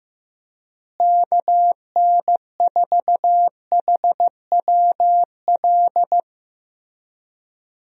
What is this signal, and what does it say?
Morse code, heard as a keyed tone, about 15 wpm, 700 Hz. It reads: KN4HWL
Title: USA Call Signs - 6 Characters - Repeated 3x - 15wpm